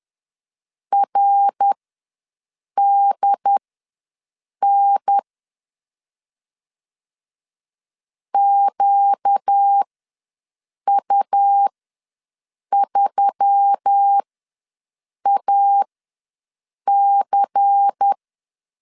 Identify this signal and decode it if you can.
Morse code, heard as a keyed tone, about 10 wpm, spaced as for 6 wpm, 800 Hz. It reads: RDN QU3AC
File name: LEZIONI / 6 PKX0 / 6 P ws 10wpm 3751